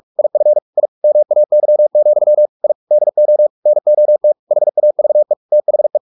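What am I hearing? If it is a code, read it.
Morse code, heard as a keyed tone, about 44 wpm, 600 Hz. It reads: IF I MAY, I DO NOT HAVE THE
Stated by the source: Morse practice at 45 wpm, the novel Never by Ken Follett in English, from CW med Gnister.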